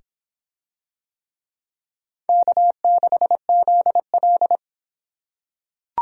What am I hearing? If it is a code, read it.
Morse code, heard as a keyed tone, about 25 wpm, 700 Hz. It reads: K6ZL